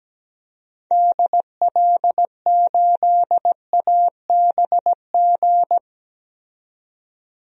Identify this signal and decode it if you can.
Morse code, heard as a keyed tone, about 17 wpm, 700 Hz. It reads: DL8ABG
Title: DXCC-100 Call Signs - 6 Characters - 17wpm